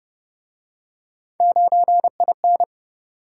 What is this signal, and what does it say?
Morse code, heard as a keyed tone, about 34 wpm, 700 Hz. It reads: 9IN